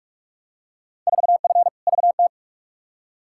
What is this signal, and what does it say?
Morse code, heard as a keyed tone, about 47 wpm, 700 Hz. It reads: 4F VT
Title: Field Day Exchanges Encore - 45wpm